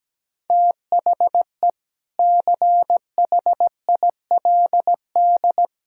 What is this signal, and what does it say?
Morse code, heard as a keyed tone, about 17 wpm, 700 Hz. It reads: THE CHILD
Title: Sentences from Top 1000 Words - Review Part 6 - 17wpm